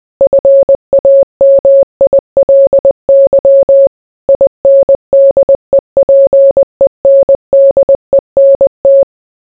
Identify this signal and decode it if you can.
Morse code, heard as a keyed tone, about 20 wpm, 550 Hz. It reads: FAMILY INDEPENDENT